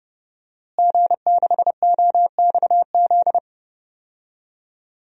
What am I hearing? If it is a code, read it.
Morse code, heard as a keyed tone, about 31 wpm, 700 Hz. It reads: G6OXZ
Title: DXCC-100 Call Signs - 5 Characters - Repeated 3x - 30wpm